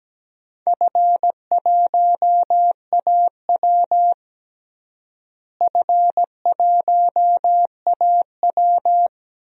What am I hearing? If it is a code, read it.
Morse code, heard as a keyed tone, about 17 wpm, 700 Hz. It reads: F1AW F1AW
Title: DXCC-100 Call Signs - 4 Characters - Repeated 2x - 17wpm